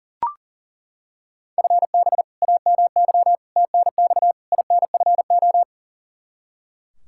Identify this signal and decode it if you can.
Morse code, heard as a keyed tone, about 40 wpm, 700 Hz. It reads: FB AMY TNX INFO
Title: Practice QSO - Exchange 3 - More Info - 40wpm